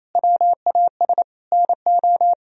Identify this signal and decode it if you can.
Morse code, heard as a keyed tone, about 34 wpm, 700 Hz. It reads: WAS NO
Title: Speed-Racing - Sets of 3 Words 28wpm